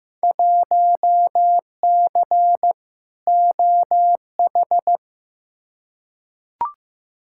Morse code, heard as a keyed tone, about 15 wpm, 700 Hz. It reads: 1C OH
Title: Field Day Exchanges - 15wpm